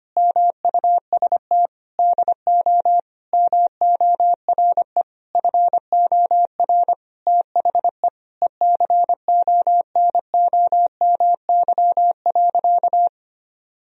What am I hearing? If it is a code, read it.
Morse code, heard as a keyed tone, about 25 wpm, 700 Hz. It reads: MUST DO MORE FOR THE ECONOMY.
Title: Sentences from Top 1000 Words - Review Part 3 - 25wpm